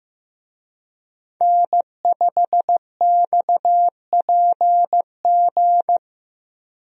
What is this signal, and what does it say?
Morse code, heard as a keyed tone, about 15 wpm, 700 Hz. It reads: N5XPG
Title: USA Call Signs - 5 Characters - Repeated 3x - 15wpm